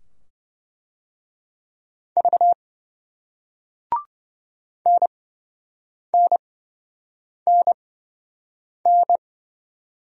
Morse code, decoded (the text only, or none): V NNNN